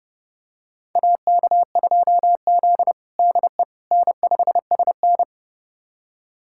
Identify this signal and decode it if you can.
Morse code, heard as a keyed tone, about 32 wpm, 700 Hz. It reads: AK2Z DE N5SN
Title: USA Callsign DE Callsign - 4 Characters - 30wpm